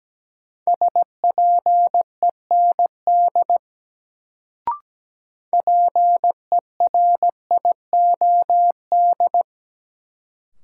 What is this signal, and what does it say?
Morse code, heard as a keyed tone, about 17 wpm, 700 Hz. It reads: SPEND PERIOD
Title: Sets of 5 Words - New Word Drill - 17wpm